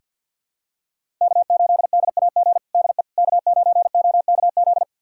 Morse code, heard as a keyed tone, about 47 wpm, 700 Hz. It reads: K8DAG DE K9OKZ
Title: USA Callsign DE Callsign - 5 Characters - 50wpm